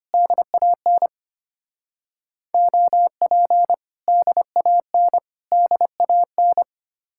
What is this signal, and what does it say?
Morse code, heard as a keyed tone, about 32 wpm, 700 Hz. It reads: DAN OP DAN DAN